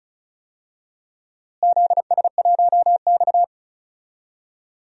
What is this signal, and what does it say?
Morse code, heard as a keyed tone, about 36 wpm, 700 Hz. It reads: ZS1X
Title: DXCC-100 Call Signs - 4 Characters - Repeated 2x - 35wpm